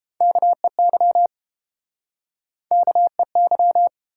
Morse code, heard as a keyed tone, about 29 wpm, 700 Hz. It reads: KEY KEY